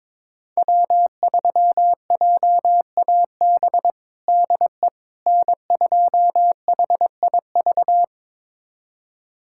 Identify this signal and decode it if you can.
Morse code, heard as a keyed tone, about 21 wpm, 700 Hz. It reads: W3JAB DE N2HIV
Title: USA Callsign DE Callsign - 5 Characters - 22wpm